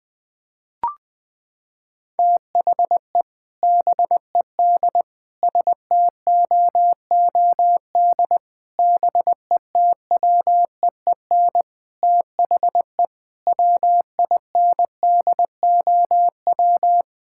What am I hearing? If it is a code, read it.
Morse code, heard as a keyed tone, about 21 wpm, 700 Hz. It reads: THE BED STOOD BETWEEN THE WINDOW
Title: Sentences from Top 1600 Words 20wpm